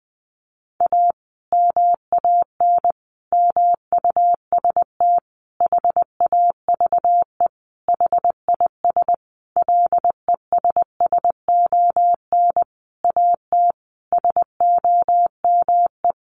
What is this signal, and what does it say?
Morse code, heard as a keyed tone, about 20 wpm, 700 Hz. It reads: A MAN MUST HAVE HIS LESSON AT SOME
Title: Sentences from Top 1200 Words 20wpm